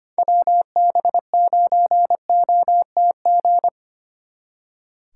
Morse code, heard as a keyed tone, about 30 wpm, 700 Hz. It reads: WB9OTG
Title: Speed Racing - USA 2x3 Call Signs 25wpm